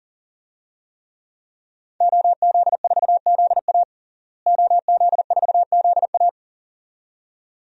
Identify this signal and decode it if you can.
Morse code, heard as a keyed tone, about 40 wpm, 700 Hz. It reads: OZ4ZA OZ4ZA
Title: DXCC-100 Call Signs - 5 Characters - Repeated 2x - 40wpm